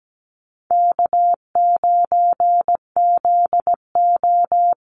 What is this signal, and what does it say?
Morse code, heard as a keyed tone, about 17 wpm, 700 Hz. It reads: K9ZO
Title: Easy Call Signs - 17wpm